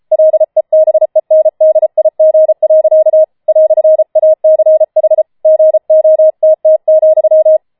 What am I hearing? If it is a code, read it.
Morse code, heard as a keyed tone, about 33 wpm, 630 Hz. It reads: LEBENDIG. "ACH GOTT,
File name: Schneewittchen Teil5 34wpm